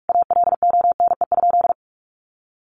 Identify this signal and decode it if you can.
Morse code, heard as a keyed tone, about 44 wpm, 700 Hz. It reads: ALONE?
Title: Sentences from Top 1100 Words 45wpm